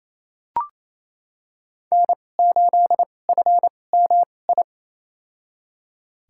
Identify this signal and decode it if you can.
Morse code, heard as a keyed tone, about 28 wpm, spaced as for 20 wpm, 700 Hz. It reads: N8FMI
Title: USA Call Signs - 5 Characters - 2x Character Spacing - 28wpm